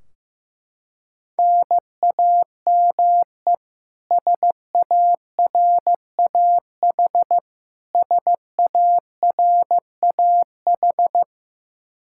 Morse code, heard as a keyed tone, about 16 wpm, 700 Hz. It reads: NAME SARAH SARAH
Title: Practice QSO - Exchange 1 - Beginning - 15wpm